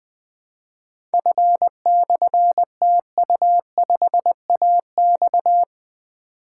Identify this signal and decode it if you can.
Morse code, heard as a keyed tone, about 20 wpm, 700 Hz. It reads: F/TU5AX
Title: Difficult Call Signs 20wpm